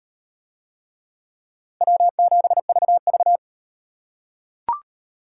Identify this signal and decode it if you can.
Morse code, heard as a keyed tone, about 35 wpm, 700 Hz. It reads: W7VV